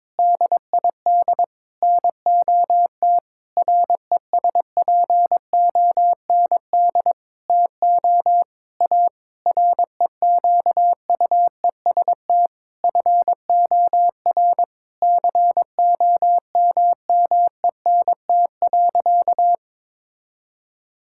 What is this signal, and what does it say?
Morse code, heard as a keyed tone, about 22 wpm, 700 Hz. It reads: DID NOT RESPOND TO A REQUEST FOR COMMENT.
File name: practice / 22wpm / Sentences with 8-Word Phrases - 22wpm